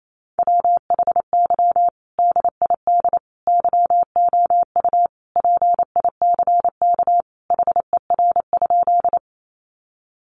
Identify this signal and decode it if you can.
Morse code, heard as a keyed tone, about 28 wpm, 700 Hz. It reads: WHY DID YOU PICK HER?
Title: Sentences from Top 1000 Words 28wpm